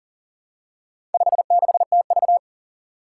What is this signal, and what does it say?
Morse code, heard as a keyed tone, about 40 wpm, 700 Hz. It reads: 56TV
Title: Four Challenging Letter-Number Combinations 40wpm